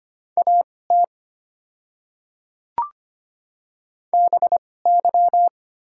Morse code, heard as a keyed tone, about 25 wpm, spaced as for 18 wpm, 700 Hz. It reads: AT BY
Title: 2x Character Spacing - Top 100 Words - 25wpm